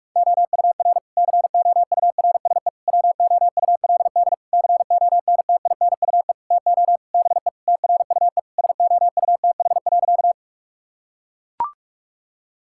Morse code, heard as a keyed tone, about 41 wpm, 700 Hz. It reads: OUR COURSE WOULD CONTINUE TO BE TRUE SOUTH.
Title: Sentences from Top 1000 Words - Review Part 6 - 45wpm